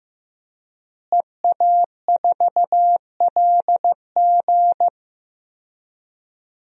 Morse code, heard as a keyed tone, about 15 wpm, 700 Hz. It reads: EA4LG
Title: DXCC-100 Call Signs - 5 Characters - Repeated 3x - 15wpm